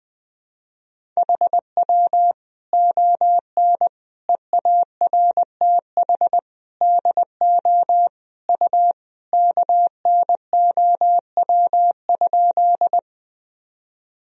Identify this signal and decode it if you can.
Morse code, heard as a keyed tone, about 20 wpm, 700 Hz. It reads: HW ON EARTH DO U KNOW?
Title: Sentences from Top 200 Words and Abbreviations 20wpm